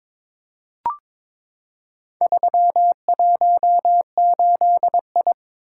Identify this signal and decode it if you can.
Morse code, heard as a keyed tone, about 21 wpm, 690 Hz. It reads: 318I
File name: practice / 22wpm / Four Letter-Number Combinations - 22wpm